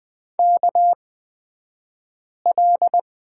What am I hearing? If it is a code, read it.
Morse code, heard as a keyed tone, about 20 wpm, spaced as for 5 wpm, 700 Hz. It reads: KL
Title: 242 Lesson 39d - Letter z - Two Characters 20wpm